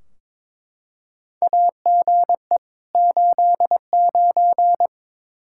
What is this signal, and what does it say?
Morse code, heard as a keyed tone, about 22 wpm, 700 Hz. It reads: AGE 89
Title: Practice QSO - Exchange 3 - More Info - 22wpm